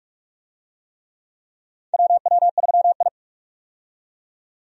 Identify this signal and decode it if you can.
Morse code, heard as a keyed tone, about 44 wpm, 700 Hz. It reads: WW3I
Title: USA Call Signs - 4 Characters - 45wpm